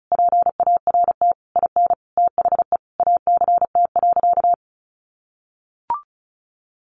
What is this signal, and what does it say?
Morse code, heard as a keyed tone, about 34 wpm, 700 Hz. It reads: PART IN THE ACT.
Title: Sentences from Top 1100 Words 35wpm